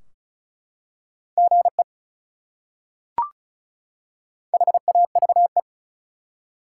Morse code, heard as a keyed tone, about 34 wpm, 700 Hz. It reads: GE HAVE
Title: Top 180 QSO Elements 35wpm